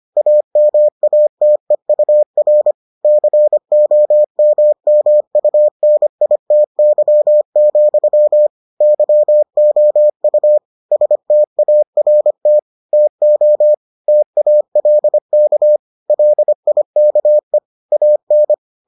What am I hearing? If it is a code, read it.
Morse code, heard as a keyed tone, about 25 wpm, 600 Hz. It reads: AMATEUR COMMUNITY, YOU START TO TALK LIKE AN